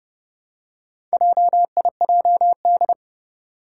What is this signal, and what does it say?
Morse code, heard as a keyed tone, about 28 wpm, 690 Hz. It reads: JIJD
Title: Four Challenging Letter-Number Combinations 30wpm